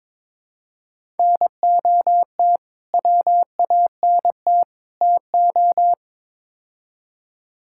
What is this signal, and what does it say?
Morse code, heard as a keyed tone, about 22 wpm, 700 Hz. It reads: NOT WANT TO
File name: practice / 22wpm / Build A Phrase - 5-Word Phrases - 22wpm